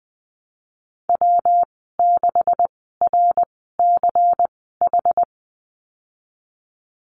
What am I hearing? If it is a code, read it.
Morse code, heard as a keyed tone, about 20 wpm, spaced as for 15 wpm, 700 Hz. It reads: W6RCH
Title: 193 Lesson 31g - Letter v - Call Signs 20wpm